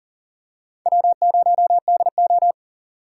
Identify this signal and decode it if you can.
Morse code, heard as a keyed tone, about 40 wpm, 700 Hz. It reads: W0DO
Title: USA Call Signs - 4 Characters - 3x Character Spacing - 40wpm